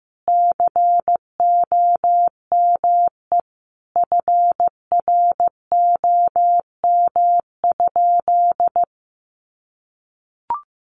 Morse code, heard as a keyed tone, about 15 wpm, 700 Hz. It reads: COME FROM?
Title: Sentences from Top 200 Words 15wpm